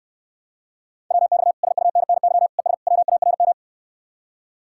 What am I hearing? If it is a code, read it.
Morse code, heard as a keyed tone, about 69 wpm, 700 Hz. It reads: WX SUNNY ES WARM